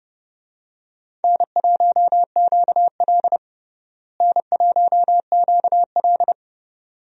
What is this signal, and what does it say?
Morse code, heard as a keyed tone, about 29 wpm, 700 Hz. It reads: N1QL N1QL
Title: USA Call Signs - 4 Characters - Repeated 2x - 30wpm